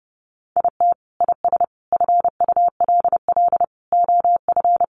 Morse code, heard as a keyed tone, about 30 wpm, 700 Hz. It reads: IT IS FULL OF
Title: Sentences from Top 1500 Words 30wpm